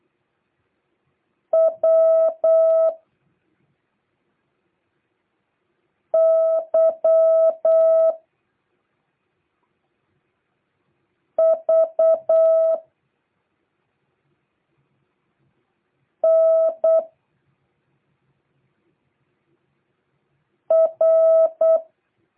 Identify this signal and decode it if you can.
Morse code, heard as a keyed tone, about 8 wpm, 650 Hz. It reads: W Y V N R